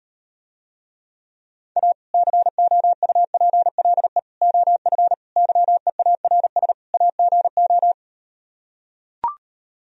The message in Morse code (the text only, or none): A COUPLE OF YEARS AGO